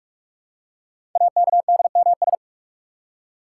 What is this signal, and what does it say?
Morse code, heard as a keyed tone, about 44 wpm, 690 Hz. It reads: AKDMS
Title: Code Groups - 5 Letters - 45wpm